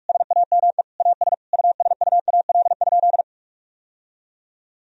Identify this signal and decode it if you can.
Morse code, heard as a keyed tone, about 46 wpm, 700 Hz. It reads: SAME AS USUAL?
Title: Sentences from Top 1000 Words - Review Part 2 - 45wpm